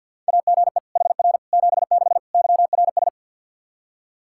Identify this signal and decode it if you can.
Morse code, heard as a keyed tone, about 47 wpm, 700 Hz. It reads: AGE HR 76 YRS
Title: Practice QSO - Exchange 3 - More Info - 50wpm